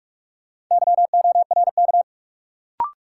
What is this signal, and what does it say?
Morse code, heard as a keyed tone, about 48 wpm, 700 Hz. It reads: YORK
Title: Sets of 2 Words - New Word Drill - 45wpm